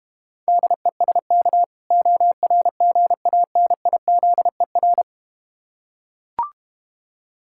Morse code, heard as a keyed tone, about 32 wpm, 700 Hz. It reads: DESK ORGANIZER